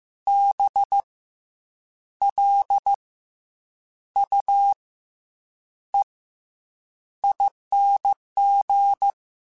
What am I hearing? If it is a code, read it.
Morse code, heard as a keyed tone, about 15 wpm, spaced as for 7 wpm, 800 Hz. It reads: BLUEING
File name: Special INGs-mp3s / Special INGs-mp3s / Fast ing @13